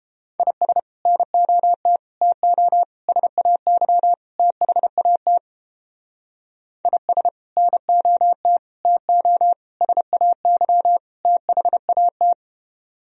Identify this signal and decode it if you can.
Morse code, heard as a keyed tone, about 29 wpm, 700 Hz. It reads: IS NOT TO SAY THAT IS NOT TO SAY THAT